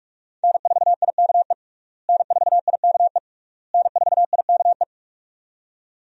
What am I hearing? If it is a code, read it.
Morse code, heard as a keyed tone, about 46 wpm, 700 Hz. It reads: N4IKE N4IKE N4IKE
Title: USA Call Signs - 5 Characters - Repeated 3x - 45wpm